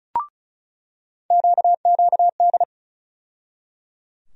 Morse code, heard as a keyed tone, about 36 wpm, 700 Hz. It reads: QQD
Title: USA Call Sign Postfixes - 3 Letters 35wpm